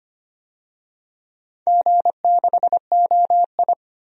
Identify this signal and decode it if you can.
Morse code, heard as a keyed tone, about 25 wpm, 700 Hz. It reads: G6OI
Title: DXCC-100 Call Signs - 4 Characters - 25wpm